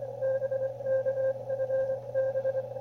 Morse code, heard as a keyed tone, about 24 wpm, 530 Hz. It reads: BK FB
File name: qrp / wb0jnr at 100mW with ae5x 02-23-2018